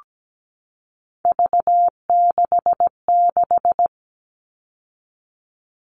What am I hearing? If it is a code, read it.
Morse code, heard as a keyed tone, about 17 wpm, 700 Hz. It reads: V66
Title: All Cut Numbers - 3-Digit 17wpm